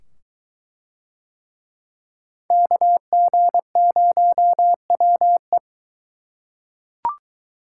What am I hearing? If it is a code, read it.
Morse code, heard as a keyed tone, about 23 wpm, 700 Hz. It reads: KG0WE